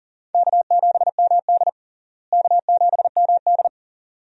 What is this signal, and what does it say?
Morse code, heard as a keyed tone, about 40 wpm, 700 Hz. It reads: K7MD K7MD
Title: USA Call Signs - 4 Characters - Repeated 2x - 40wpm